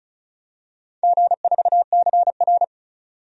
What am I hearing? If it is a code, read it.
Morse code, heard as a keyed tone, about 34 wpm, 700 Hz. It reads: G4CR
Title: DXCC-100 Call Signs - 4 Characters - 35wpm